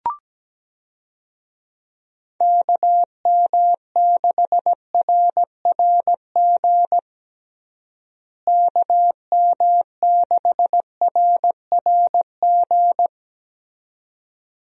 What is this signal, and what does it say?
Morse code, heard as a keyed tone, about 17 wpm, 700 Hz. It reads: KM6RRG KM6RRG
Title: USA Call Signs - 6 Characters - Repeated 2x - 17wpm